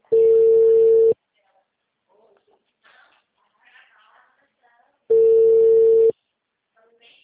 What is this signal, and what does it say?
Ringback tone (China, 450 Hz), with speech.
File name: out- -2004- - - .4